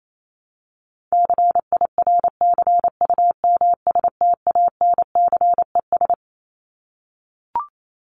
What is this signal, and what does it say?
Morse code, heard as a keyed tone, about 28 wpm, 700 Hz. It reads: CIRCUMSTANCES
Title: ES:S Suffix - Top 100 Words with Base Words 28wpm